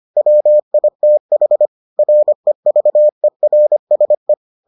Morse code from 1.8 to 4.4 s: REVERSE

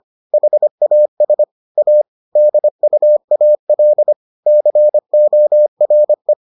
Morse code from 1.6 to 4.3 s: A DUAL

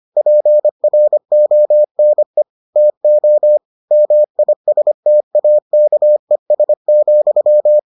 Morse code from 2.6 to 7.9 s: TO MISTAKES,

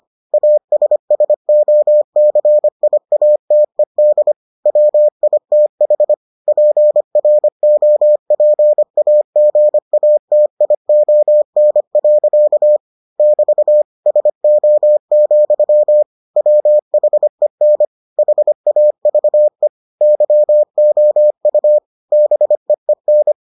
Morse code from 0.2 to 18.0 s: ASSOCIATED WITH PROPAGATION. <BT> SO, WHEN

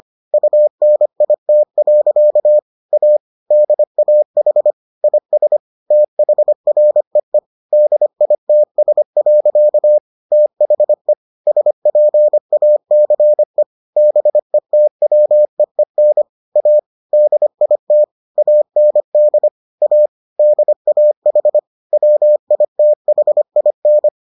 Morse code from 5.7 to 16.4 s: THREE DITS. THE SPACE BETWEEN

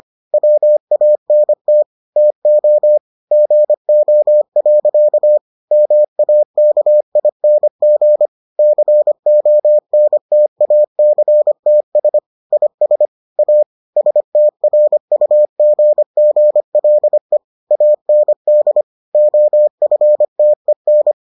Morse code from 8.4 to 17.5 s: CONTACTS IS A STRUGGLE

